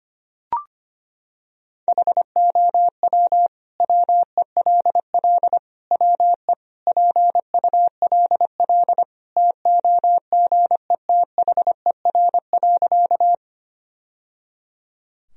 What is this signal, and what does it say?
Morse code, heard as a keyed tone, about 25 wpm, 700 Hz. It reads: HOW WELL WE PULL TOGETHER.
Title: Sentences from Top 1000 Words - Review Part 2 - 25wpm